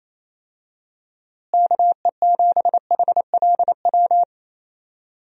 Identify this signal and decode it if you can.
Morse code, heard as a keyed tone, about 28 wpm, 700 Hz. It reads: KE7HLW
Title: USA Call Signs - 6 Characters - Repeated 2x - 28wpm